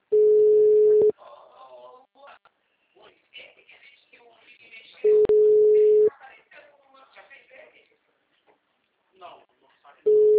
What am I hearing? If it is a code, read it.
Ringback tone (Europe, 425 Hz), with speech.
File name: out- -2018